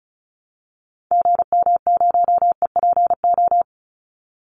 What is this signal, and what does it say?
Morse code, heard as a keyed tone, about 34 wpm, 700 Hz. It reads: GM0EPO
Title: Call Signs - 35wpm